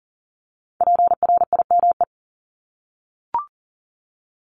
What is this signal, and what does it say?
Morse code, heard as a keyed tone, about 42 wpm, 700 Hz. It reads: PRIME
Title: Multi-letter Phonemes Repeated 3x 40wpm